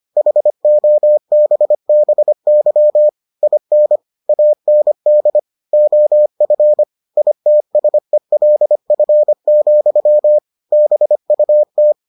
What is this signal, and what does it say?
Morse code, heard as a keyed tone, about 25 wpm, 600 Hz. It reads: HOBBY IN AND OF ITSELF, BUT